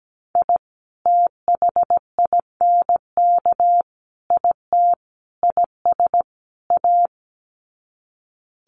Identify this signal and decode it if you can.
Morse code, heard as a keyed tone, about 17 wpm, 700 Hz. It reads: I THINK IT IS A
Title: Sets of 5 Words Encore 17wpm